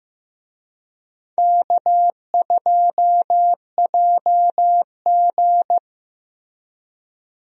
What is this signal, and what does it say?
Morse code, heard as a keyed tone, about 15 wpm, 700 Hz. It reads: K2JG
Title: USA Call Signs - 4 Characters - Repeated 2x - 15wpm